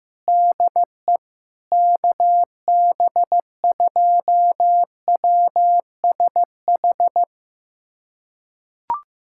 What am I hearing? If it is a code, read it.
Morse code, heard as a keyed tone, about 15 wpm, 700 Hz. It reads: DE KB2WSH
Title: USA Callsign DE Callsign 15wpm